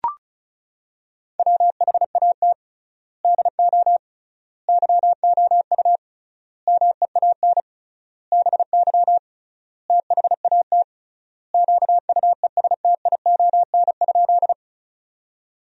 Morse code, heard as a keyed tone, about 34 wpm, 700 Hz. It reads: WHAT DO YOU MEAN BY THAT QUESTION?
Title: 3x Word-Spacing - Sentences from Top 400 Words - 35wpm